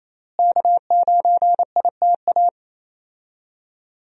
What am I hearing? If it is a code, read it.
Morse code, heard as a keyed tone, about 28 wpm, 700 Hz. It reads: K9ITA
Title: USA Call Signs - 5 Characters - Repeated 2x - 28wpm